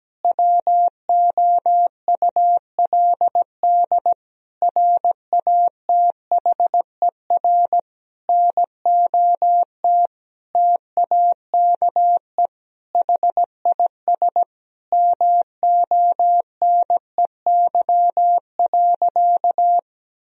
Morse code, heard as a keyed tone, about 17 wpm, 700 Hz. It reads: WOULD RATHER NOT TAKE HIS MONEY.
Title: Sentences from Top 1000 Words - Review Part 4 - 17wpm